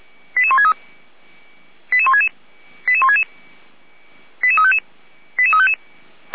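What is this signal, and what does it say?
Five-tone selective call (ZVEI), heard as quick runs of five tones, tones of about 70 ms. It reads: 88163 88166 88166 88355 88355
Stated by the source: ZVEI five-tone selective call 88191 received by SDR.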